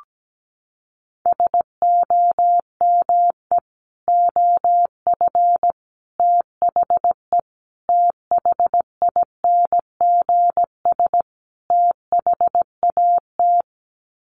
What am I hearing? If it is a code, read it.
Morse code, heard as a keyed tone, about 17 wpm, 700 Hz. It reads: SOME OF THE THINGS THAT
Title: Sets of 5 Words 17wpm